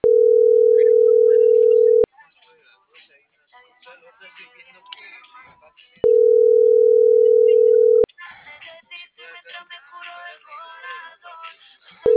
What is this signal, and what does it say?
Ringback tone (North America), 440 + 480 Hz.